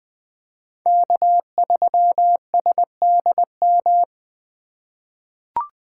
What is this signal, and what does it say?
Morse code, heard as a keyed tone, about 20 wpm, 700 Hz. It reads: K3SDM
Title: 2023 POTA Hunter Call Signs - 20wpm